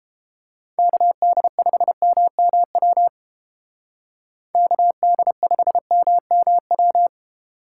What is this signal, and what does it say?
Morse code, heard as a keyed tone, about 32 wpm, 700 Hz. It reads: KD5MMW KD5MMW